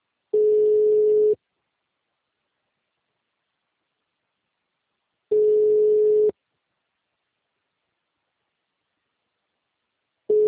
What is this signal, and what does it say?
Ringback tone (Europe, 425 Hz).
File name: out- -2002- - - .96629